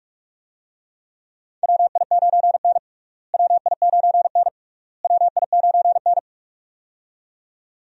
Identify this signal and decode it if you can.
Morse code, heard as a keyed tone, about 46 wpm, 700 Hz. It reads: WI9N WI9N WI9N
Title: USA Call Signs - 4 Characters - Repeated 3x - 45wpm